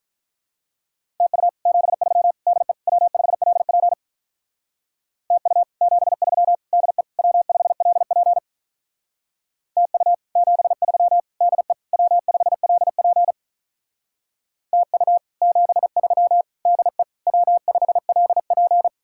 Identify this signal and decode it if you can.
Morse code, heard as a keyed tone, about 43 wpm, 700 Hz. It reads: TU 73 DE W5LP TU 73 DE W5LP TU 73 DE W5LP TU 73 DE W5LP
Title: Speed-Racing Common QSO Phrases - 35wpm